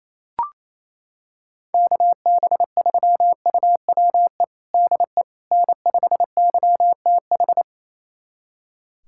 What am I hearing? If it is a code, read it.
Morse code, heard as a keyed tone, about 28 wpm, 700 Hz. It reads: KB3UWE DE N5YTH
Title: USA Callsign DE Callsign 28wpm